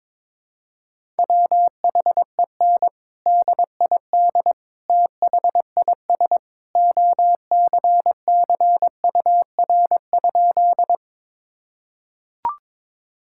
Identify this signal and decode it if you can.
Morse code, heard as a keyed tone, about 22 wpm, 700 Hz. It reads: WHEN DID THIS OCCUR?
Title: Sentences from Top 1000 Words - Review Part 6 - 22wpm